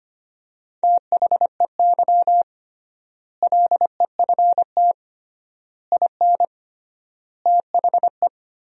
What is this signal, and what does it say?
Morse code, heard as a keyed tone, about 25 wpm, 700 Hz. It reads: THEY LEFT IN THE
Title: 3x Word-Spacing - Sentences from Top 400 Words - 25wpm